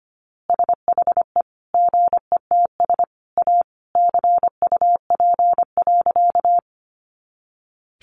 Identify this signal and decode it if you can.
Morse code, heard as a keyed tone, about 25 wpm, 700 Hz. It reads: SHE GETS A CUP.
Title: Easy Phrases 25wpm